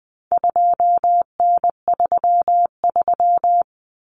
Morse code, heard as a keyed tone, about 20 wpm, 700 Hz. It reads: 2N33
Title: Common Cut Numbers - 4-Digit 20wpm